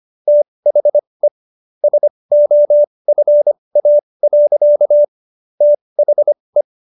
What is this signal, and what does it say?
Morse code, heard as a keyed tone, about 25 wpm, spaced as for 20 wpm, 590 Hz. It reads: THE SOFA. THE